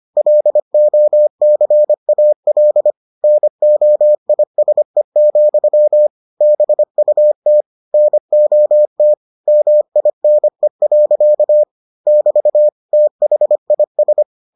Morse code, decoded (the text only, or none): LOCAL NOISE, BUT NOT MINE. <BT> THIS